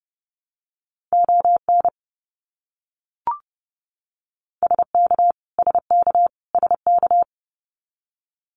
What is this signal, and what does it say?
Morse code, heard as a keyed tone, about 30 wpm, 700 Hz. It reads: ON SK SK SK
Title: Canadian Provinces and Territories - Repeated 3x - 30wpm